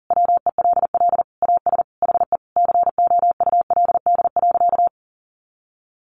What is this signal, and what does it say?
Morse code, heard as a keyed tone, about 40 wpm, 700 Hz. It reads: WELL AS HE COULD.
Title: Sentences from Top 500 Words 40wpm